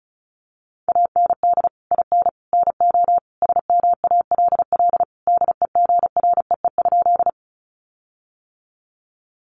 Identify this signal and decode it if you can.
Morse code, heard as a keyed tone, about 34 wpm, 700 Hz. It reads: AND IN NO SMALL DEGREE?
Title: Sentences from Top 1500 Words 35wpm